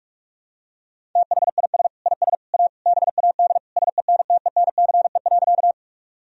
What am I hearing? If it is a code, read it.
Morse code, heard as a keyed tone, about 46 wpm, 700 Hz. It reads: THIS IS A BAD SENTENCE.
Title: Sentences from Top 1000 Words - Review Part 1 - 45wpm